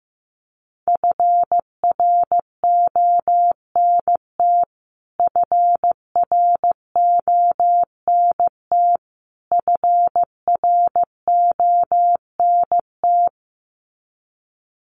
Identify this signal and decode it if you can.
Morse code, heard as a keyed tone, about 15 wpm, 700 Hz. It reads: FRONT FRONT FRONT
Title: Top 500 Words Repeated 15wpm